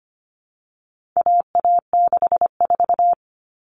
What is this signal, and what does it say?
Morse code, heard as a keyed tone, about 25 wpm, 700 Hz. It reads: AA64
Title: All Cut Numbers - 4-Digit 25wpm